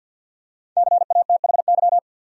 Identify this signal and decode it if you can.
Morse code, heard as a keyed tone, about 50 wpm, 700 Hz. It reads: CATHY
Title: USA Common Names - 50wpm